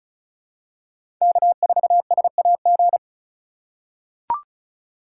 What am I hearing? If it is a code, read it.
Morse code, heard as a keyed tone, about 36 wpm, 700 Hz. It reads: K4SAG
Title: USA Call Signs - 5 Characters - 3x Character Spacing - 35wpm